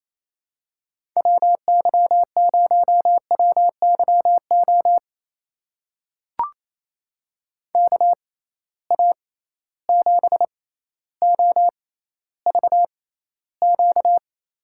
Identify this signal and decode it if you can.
Morse code, heard as a keyed tone, about 28 wpm, spaced as for 14 wpm, 700 Hz. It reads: WY0WYO KA7OVQ
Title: USA Call Signs - 6 Characters - 6x Character Spacing - 28wpm